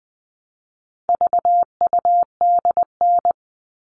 Morse code, heard as a keyed tone, about 20 wpm, 700 Hz. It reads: VUDN
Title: All Cut Numbers - 4-Digit 20wpm